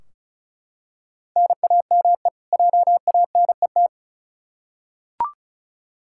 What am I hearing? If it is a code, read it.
Morse code, heard as a keyed tone, about 34 wpm, 700 Hz. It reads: NAME JANET
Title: K1USN SST QSO Practice 35wpm